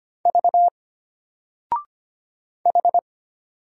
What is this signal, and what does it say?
Morse code, heard as a keyed tone, about 25 wpm, 700 Hz. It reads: V H
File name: Single-Letter-Number-25wpm